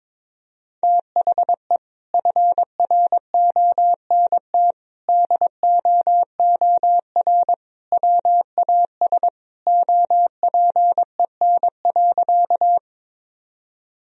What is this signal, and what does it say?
Morse code, heard as a keyed tone, about 21 wpm, 700 Hz. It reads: THE FRONT DOOR WAS OPEN.
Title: Sentences from Top 1000 Words - Review Part 2 - 22wpm